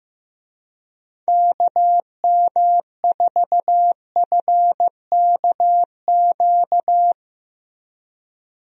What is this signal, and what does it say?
Morse code, heard as a keyed tone, about 16 wpm, 700 Hz. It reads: KM4FKQ
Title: USA Call Signs - 6 Characters - Repeated 2x - 15wpm